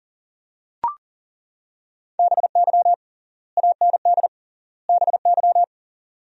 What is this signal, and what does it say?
Morse code, heard as a keyed tone, about 37 wpm, 700 Hz. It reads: BY AND BY
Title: 3x Word-Spacing - Sentences from Top 300 Words - 40wpm